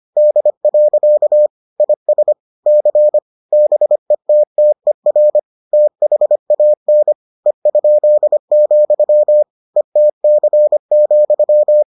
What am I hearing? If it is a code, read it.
Morse code, heard as a keyed tone, about 25 wpm, 600 Hz. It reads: D. IS C BETTER THAN E?, ETC,